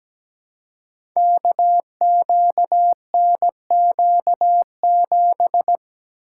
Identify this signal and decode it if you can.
Morse code, heard as a keyed tone, about 17 wpm, 700 Hz. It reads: KQNQ7
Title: Code Groups - 5 Letter:Numbers - 17wpm